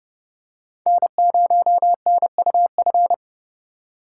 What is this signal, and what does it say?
Morse code, heard as a keyed tone, about 30 wpm, 700 Hz. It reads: N0NUF
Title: USA Call Signs - 4x Character Spacing - 30wpm